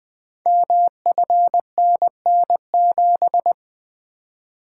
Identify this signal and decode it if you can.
Morse code, heard as a keyed tone, about 20 wpm, 700 Hz. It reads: MFNN7
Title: Code Groups - 5 Letter:Numbers - 20wpm